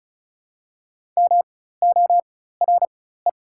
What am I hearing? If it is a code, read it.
Morse code, heard as a keyed tone, about 36 wpm, spaced as for 16 wpm, 700 Hz. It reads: MORE
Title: 4x Character Spacing - Top 100 Words - 35wpm